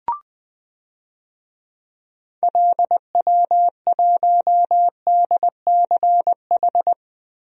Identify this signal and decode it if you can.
Morse code, heard as a keyed tone, about 20 wpm, 700 Hz. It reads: LW1DCH
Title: DXCC-100 Call Signs - 6 Characters - Repeated 2x - 20wpm